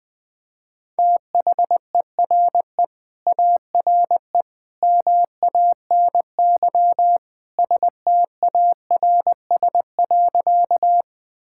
Morse code, heard as a keyed tone, about 21 wpm, 700 Hz. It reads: THERE ARE MANY STARS.
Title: Sentences with 2-Word Phrases - 20wpm